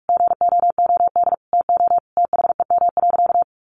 Morse code, heard as a keyed tone, about 44 wpm, 700 Hz. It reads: GOOD TO THEM.
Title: Sentences from Top 1200 Words 45wpm